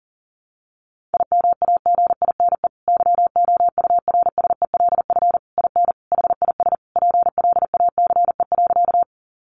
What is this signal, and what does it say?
Morse code, heard as a keyed tone, about 40 wpm, 700 Hz. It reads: IMAGINE YOURSELF IN HIS PLACE.
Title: Sentences from Top 1400 Words 40wpm